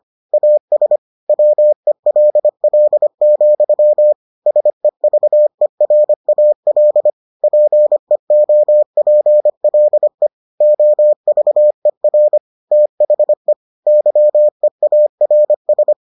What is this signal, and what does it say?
Morse code, heard as a keyed tone, about 25 wpm, 600 Hz. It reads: AS WELL, SEVERAL PEOPLE OVER THE YEARS